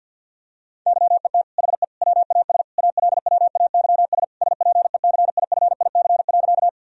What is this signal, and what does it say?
Morse code, heard as a keyed tone, about 51 wpm, 700 Hz. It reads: YET HE WAS ALWAYS SPECIFIC.
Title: Sentences from Top 1000 Words - Review Part 3 - 50wpm